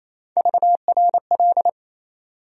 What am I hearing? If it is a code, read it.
Morse code, heard as a keyed tone, about 28 wpm, 700 Hz. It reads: VRL
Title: USA Call Sign Postfixes - 3 Letters 28wpm